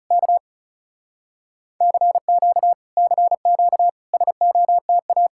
Morse code, heard as a keyed tone, about 34 wpm, 700 Hz. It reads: K CQ CQ SOTA
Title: Speed-Racing Common QSO Phrases Encore - 30wpm